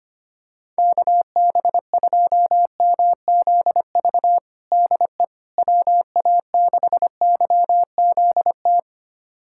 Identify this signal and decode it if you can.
Morse code, heard as a keyed tone, about 25 wpm, 700 Hz. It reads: KB2MZV DE WA6YZT
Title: USA Callsign DE Callsign - 6 Characters - 25wpm